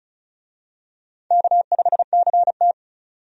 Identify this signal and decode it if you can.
Morse code, heard as a keyed tone, about 34 wpm, 700 Hz. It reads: K5CT
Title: USA Call Signs - 4 Characters - 35wpm